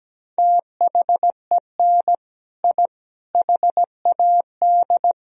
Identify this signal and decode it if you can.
Morse code, heard as a keyed tone, about 17 wpm, 700 Hz. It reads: THEN I HAD
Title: Sentences from Top 1000 Words - Review Part 1 - 17wpm